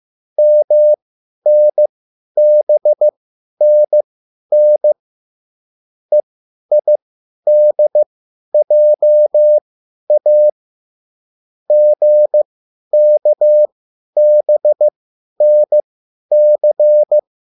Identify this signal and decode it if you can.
Morse code, heard as a keyed tone, about 15 wpm, spaced as for 10 wpm, 600 Hz. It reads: MNBNN EIDJA GKBNC
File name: QSL / CW lesson